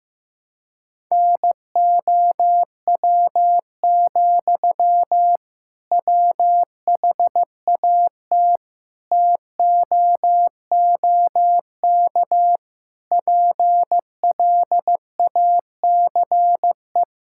Morse code, heard as a keyed tone, about 15 wpm, 700 Hz. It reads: NOW, WHAT TOOK PLACE
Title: Sentences from Top 1600 Words 15wpm